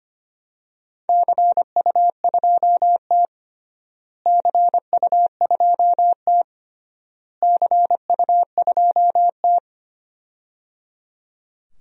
Morse code, heard as a keyed tone, about 25 wpm, 700 Hz. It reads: CU2T CU2T CU2T
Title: DXCC-100 Call Signs - 4 Characters - Repeated 3x - 25wpm